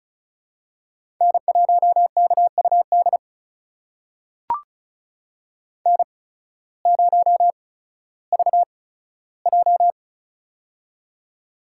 Morse code, heard as a keyed tone, about 36 wpm, spaced as for 16 wpm, 700 Hz. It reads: N1KUD N0VJ
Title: USA Call Signs - 8x Character Spacing - 35wpm